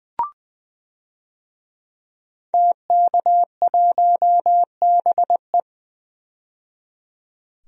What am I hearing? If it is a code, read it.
Morse code, heard as a keyed tone, about 20 wpm, 700 Hz. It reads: TK1BE
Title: DXCC-100 Call Signs - 5 Characters - 20wpm